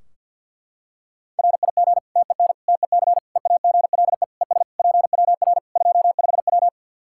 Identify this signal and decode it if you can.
Morse code, heard as a keyed tone, about 51 wpm, 700 Hz. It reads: RIG TEN TEC EAGLE ES PWR 25W
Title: K1USN SST QSO Practice 50wpm